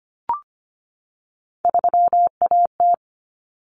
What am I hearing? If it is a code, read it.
Morse code, heard as a keyed tone, about 25 wpm, 700 Hz. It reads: 3AT
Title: Common Cut Numbers - 3-Digit 25wpm